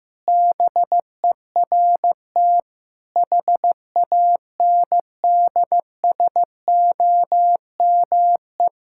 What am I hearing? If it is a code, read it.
Morse code, heard as a keyed tone, about 16 wpm, 700 Hz. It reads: BERT HANDSOME
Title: Pangram Challenge v2 - 15wpm